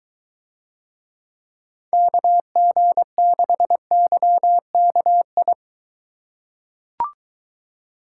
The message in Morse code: KG6YKI

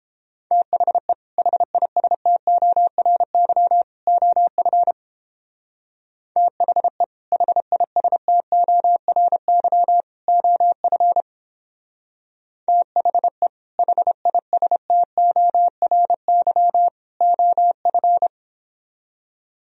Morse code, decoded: THE HISTORY OF THE HISTORY OF THE HISTORY OF

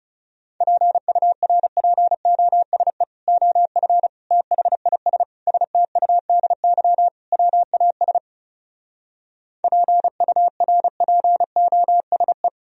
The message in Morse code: PURPOSE OF THIS STUDY WAS PURPOSE

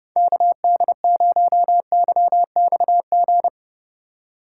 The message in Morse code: KD0YXG

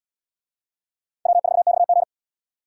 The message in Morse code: W4XG

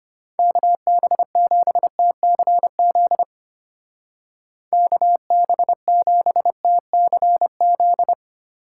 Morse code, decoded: KB7TCZ KB7TCZ

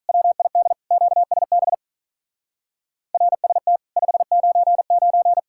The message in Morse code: WID QSB RST 599

